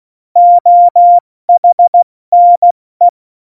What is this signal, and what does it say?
Morse code, heard as a keyed tone, about 16 wpm, 700 Hz. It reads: OHNE
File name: Morsekurs / Witze / Witz 12. Dezember 2017, 70 F80